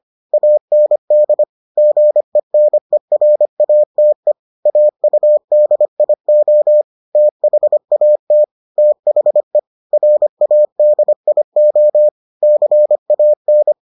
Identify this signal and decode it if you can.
Morse code, heard as a keyed tone, about 25 wpm, 600 Hz. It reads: AND GENERATE AUDIO THAT THE RADIO CAN